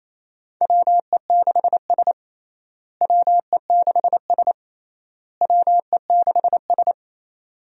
Morse code, heard as a keyed tone, about 27 wpm, 700 Hz. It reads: WE6S WE6S WE6S
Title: USA Call Signs - 4 Characters - Repeated 3x - 28wpm